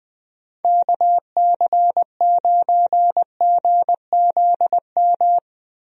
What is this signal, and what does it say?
Morse code, heard as a keyed tone, about 20 wpm, 700 Hz. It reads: KC9GZM
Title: USA Call Signs - 6 Characters - Repeated 3x - 20wpm